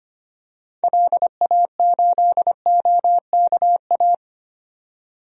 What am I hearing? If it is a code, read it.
Morse code, heard as a keyed tone, about 25 wpm, 700 Hz. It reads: LA8OKA
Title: DXCC-100 Call Signs - 6 Characters - 25wpm